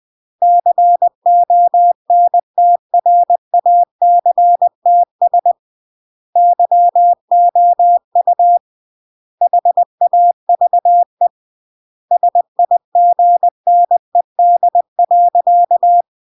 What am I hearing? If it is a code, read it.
Morse code, heard as a keyed tone, about 20 wpm, 700 Hz. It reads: CONTRACTS YOU HAVE SIGNED.
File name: CW / jodorowsky / 82 commandments 20-20